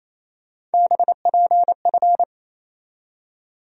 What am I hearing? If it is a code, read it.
Morse code, heard as a keyed tone, about 27 wpm, 700 Hz. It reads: BPF
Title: USA Call Sign Postfixes - 3 Letters 28wpm